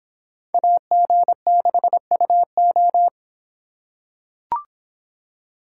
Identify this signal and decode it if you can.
Morse code, heard as a keyed tone, about 26 wpm, 700 Hz. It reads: AG6UO